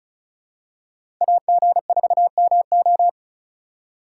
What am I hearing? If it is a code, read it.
Morse code, heard as a keyed tone, about 34 wpm, 700 Hz. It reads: AG4MO
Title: USA Call Signs - 5 Characters - 2x Character Spacing - 35wpm